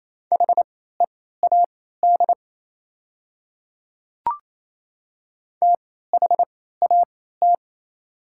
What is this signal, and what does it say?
Morse code, heard as a keyed tone, about 28 wpm, spaced as for 16 wpm, 700 Hz. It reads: HEAD THAT
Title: 3x Character Spacing - Top 300 Words - 28wpm